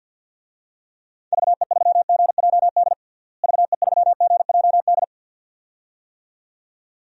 Morse code, heard as a keyed tone, about 50 wpm, 700 Hz. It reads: VE3GJD VE3GJD
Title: DXCC-100 Call Signs - 6 Characters - Repeated 2x - 50wpm